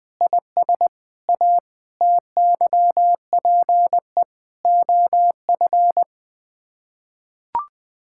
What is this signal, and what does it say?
Morse code, heard as a keyed tone, about 20 wpm, 700 Hz. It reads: IS A TYPE OF
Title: Sets of 8 Words - 20wpm